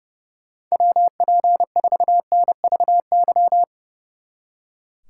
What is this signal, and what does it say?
Morse code, heard as a keyed tone, about 35 wpm, 700 Hz. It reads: WP4NVY
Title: Speed Racing - USA 2x3 Call Signs 30wpm